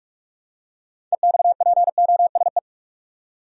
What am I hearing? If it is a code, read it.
Morse code, heard as a keyed tone, about 47 wpm, 700 Hz. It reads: EXPOSE
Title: Six Letter Words - 45wpm